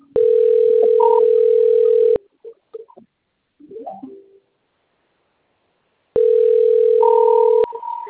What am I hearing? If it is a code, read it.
Ringback tone (North America), 440 + 480 Hz.